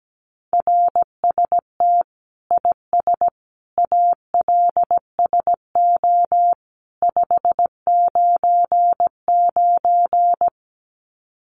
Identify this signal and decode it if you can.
Morse code, heard as a keyed tone, about 17 wpm, 700 Hz. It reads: RST IS ALSO 599
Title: Common QSO Phrases 17wpm